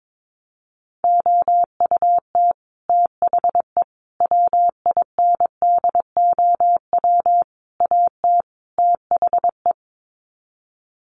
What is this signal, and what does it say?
Morse code, heard as a keyed tone, about 22 wpm, 700 Hz. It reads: OUT THE WINDOW AT THE
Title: Sets of 5 Words Encore2 22wpm